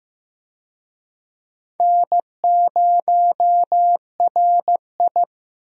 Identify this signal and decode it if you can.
Morse code, heard as a keyed tone, about 15 wpm, 700 Hz. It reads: N0RI
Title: USA Call Signs - 4 Characters - Repeated 3x - 15wpm